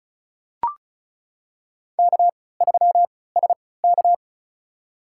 Morse code, heard as a keyed tone, about 34 wpm, spaced as for 20 wpm, 700 Hz. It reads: K3SK
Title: USA Call Signs - 4 Characters - 3x Character Spacing - 35wpm